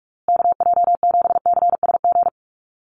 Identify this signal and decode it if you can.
Morse code, heard as a keyed tone, about 44 wpm, 700 Hz. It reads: KJ7CSG
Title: USA Call Signs Repeated 2x 45wpm